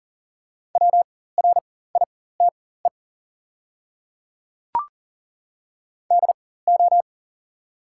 Morse code, heard as a keyed tone, about 40 wpm, spaced as for 19 wpm, 690 Hz. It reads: WRITE DO
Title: 4x Character Spacing - Top 200 Words - 40wpm